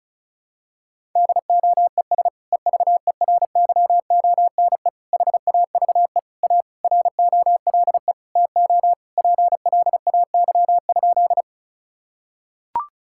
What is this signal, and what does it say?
Morse code, heard as a keyed tone, about 34 wpm, 700 Hz. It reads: DOES EVERYONE HAVE A ROLE TO PLAY?
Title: Sentences from Top 1000 Words - Review Part 6 - 35wpm